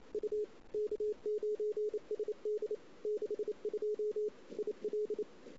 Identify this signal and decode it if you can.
Morse code, heard as a keyed tone, about 27 wpm, 420 Hz. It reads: U K9SD 62 IL
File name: CW / 2008-ss-cw-k5tr-op-n5rz -14:00